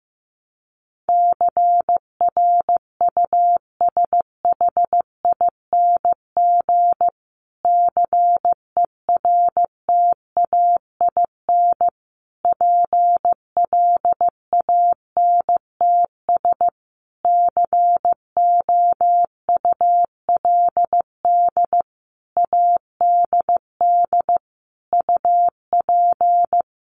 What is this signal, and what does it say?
Morse code, heard as a keyed tone, about 15 wpm, 700 Hz. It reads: CRUSHING CERTAIN PLANTS COULD ADD UP
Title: A Sound Of Thunder 15wpm